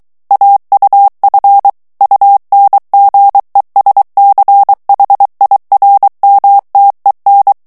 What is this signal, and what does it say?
Morse code, heard as a keyed tone, about 23 wpm, 800 Hz. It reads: AUF UNGESCHIRMTEN
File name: morse mp3 / ALLEMAND texte clair 1 vitesse 22 mots minute